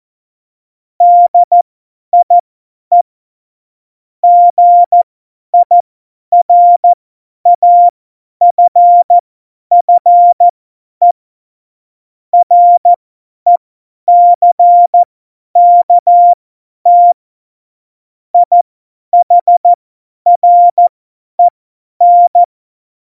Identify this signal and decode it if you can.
Morse code, heard as a keyed tone, about 14 wpm, spaced as for 10 wpm, 700 Hz. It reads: DIE GIRAFFE RECKT IHREN
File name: Morsekurs / Witze / Witz 15. Mai 2018, 50 F70